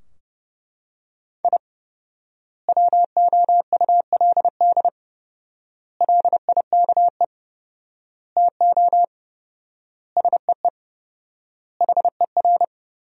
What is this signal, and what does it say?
Morse code, heard as a keyed tone, about 30 wpm, 700 Hz. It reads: I WOULD LIKE TO SEE HER
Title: 4x Word-Spacing - Sentences from Top 200 Words - 30wpm